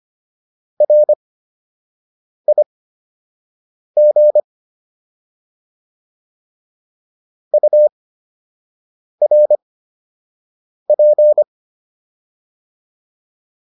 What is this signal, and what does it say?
Morse code, heard as a keyed tone, about 25 wpm, spaced as for 6 wpm, 600 Hz. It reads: RIG URP